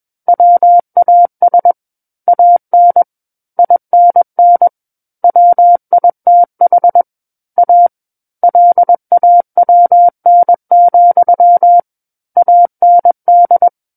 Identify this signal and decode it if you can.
Morse code, heard as a keyed tone, about 21 wpm, 700 Hz. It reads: WAS AN INN WITH A LAWN, AND
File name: Book One / 12-Chapter 12 - 21 WPM